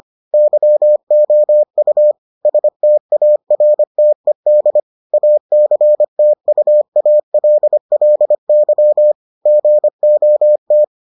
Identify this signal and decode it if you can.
Morse code, heard as a keyed tone, about 25 wpm, 600 Hz. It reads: YOU STARTED ACTUALLY GOT